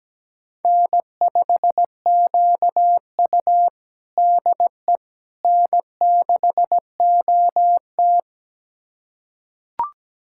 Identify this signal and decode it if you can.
Morse code, heard as a keyed tone, about 17 wpm, 700 Hz. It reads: N5QU DE N6OT
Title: USA Callsign DE Callsign - 4 Characters - 17wpm